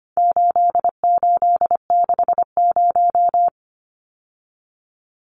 Morse code, heard as a keyed tone, about 25 wpm, 700 Hz. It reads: 8860
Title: Numbers - 4-Digit 25wpm